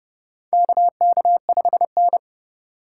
Morse code, heard as a keyed tone, about 29 wpm, 700 Hz. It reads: KK5N
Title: USA Call Signs - 4 Characters - 3x Character Spacing - 30wpm